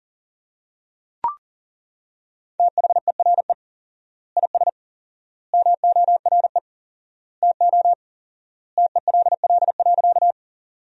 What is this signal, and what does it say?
Morse code, heard as a keyed tone, about 40 wpm, 700 Hz. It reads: THERE IS MORE TO TELL.
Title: 4x Word-Spacing - Sentences from Top 200 Words - 40wpm